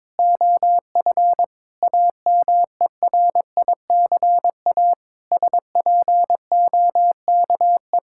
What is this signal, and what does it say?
Morse code, heard as a keyed tone, about 22 wpm, 700 Hz. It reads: OF AMERICA SPOKE
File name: practice / 22wpm / Sentences with 8-Word Phrases - 22wpm